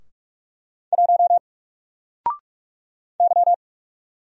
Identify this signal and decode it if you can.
Morse code, heard as a keyed tone, about 45 wpm, 700 Hz. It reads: 1 Y
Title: Single Letter-Number 45wpm